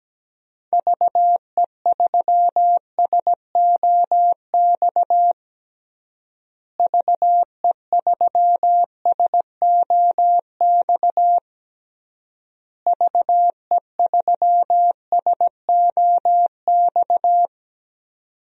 Morse code, heard as a keyed tone, about 17 wpm, 700 Hz. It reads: VE3SOX VE3SOX VE3SOX
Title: DXCC-100 Call Signs - 6 Characters - Repeated 3x - 17wpm